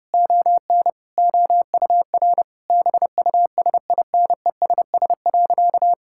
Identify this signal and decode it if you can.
Morse code, heard as a keyed tone, about 30 wpm, 700 Hz. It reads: ON OUR BUSINESS.
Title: Sentences from Top 1000 Words - Review Part 6 - 30wpm